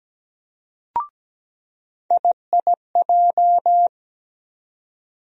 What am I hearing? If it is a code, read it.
Morse code, heard as a keyed tone, about 17 wpm, 700 Hz. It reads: IIJ
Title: USA Call Sign Postfixes - 3 Letters 17wpm